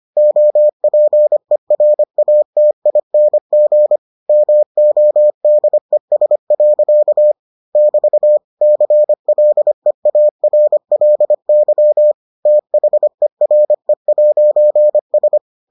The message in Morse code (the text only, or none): OPERATING MODES. <BT> CLEARLY THERE'S